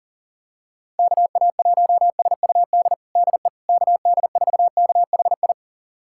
Morse code, heard as a keyed tone, about 40 wpm, 700 Hz. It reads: KA1SUD DE KD4KHI
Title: USA Callsign DE Callsign - 6 Characters - 40wpm